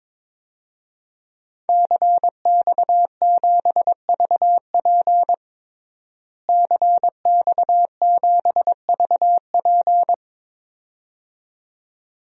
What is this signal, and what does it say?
Morse code, heard as a keyed tone, about 22 wpm, 700 Hz. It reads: CX7VP CX7VP
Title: DXCC-100 Call Signs - 5 Characters - Repeated 2x - 22wpm